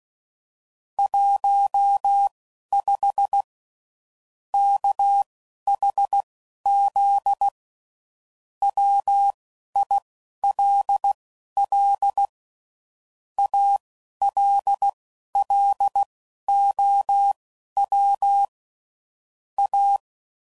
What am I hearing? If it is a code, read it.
Morse code, heard as a keyed tone, about 16 wpm, spaced as for 11 wpm, 800 Hz. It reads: 15 KHZ WILL ALLOW A
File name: CW Code Groups @10WPM / Technical Subjects / SwPwrSup